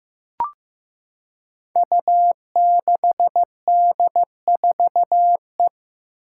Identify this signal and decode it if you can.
Morse code, heard as a keyed tone, about 15 wpm, 700 Hz. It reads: U6D4E
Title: Five Challenging Letter-Number Combinations 15wpm